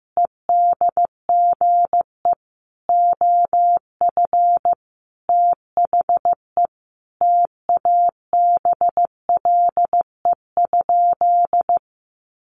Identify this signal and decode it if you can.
Morse code, heard as a keyed tone, about 15 wpm, 700 Hz. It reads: EDGE OF THE TABLE?
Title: Sentences from Top 1300 Words 15wpm